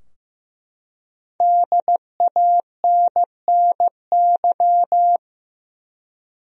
Morse code, heard as a keyed tone, about 15 wpm, 700 Hz. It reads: DANNY
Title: USA Common Names - Max 5 Letters - 15wpm